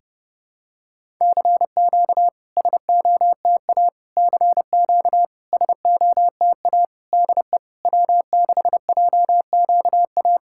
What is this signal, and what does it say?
Morse code, heard as a keyed tone, about 30 wpm, 700 Hz. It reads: CQ SOTA CQ SOTA DE W6JQA
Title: SOTA QSO Practice 30wpm